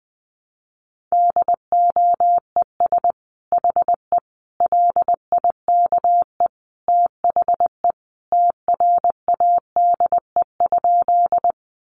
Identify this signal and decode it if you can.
Morse code, heard as a keyed tone, about 20 wpm, 700 Hz. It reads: DOES HE LIKE THE TRADE?
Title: Sentences from Top 600 Words 20wpm